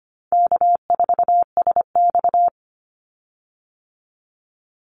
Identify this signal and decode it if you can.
Morse code, heard as a keyed tone, about 25 wpm, 700 Hz. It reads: K4SX
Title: Call Signs Repeated 2x - 25wpm